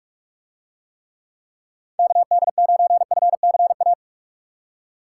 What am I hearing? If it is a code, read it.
Morse code, heard as a keyed tone, about 41 wpm, 700 Hz. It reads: KD9FCA